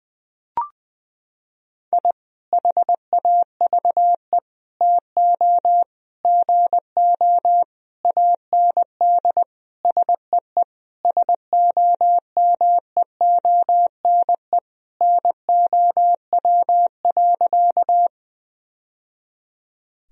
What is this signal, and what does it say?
Morse code, heard as a keyed tone, about 20 wpm, 700 Hz. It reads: I HAVE TO GO AND SEE SOMEONE NOW.
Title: Sentences from Top 1000 Words - Review Part 4 - 20wpm